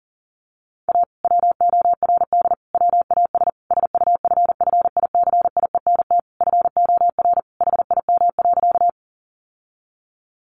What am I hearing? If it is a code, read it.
Morse code, heard as a keyed tone, about 40 wpm, 700 Hz. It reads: A WORD WAS SUFFICIENT FOR HIM.
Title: Sentences from Top 1400 Words 40wpm